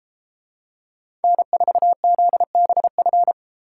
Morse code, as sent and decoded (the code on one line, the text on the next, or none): -. ....- --.. -... ..-.
N4ZBF